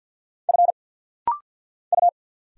Morse code, heard as a keyed tone, about 50 wpm, 700 Hz. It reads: F U
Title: Single Letters - Rapid-Fire 50wpm